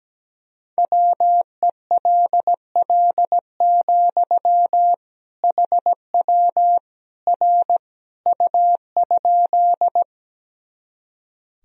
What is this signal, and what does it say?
Morse code, heard as a keyed tone, about 17 wpm, 700 Hz. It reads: WELL, HW R U?
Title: Sentences from Top 200 Words and Abbreviations 17wpm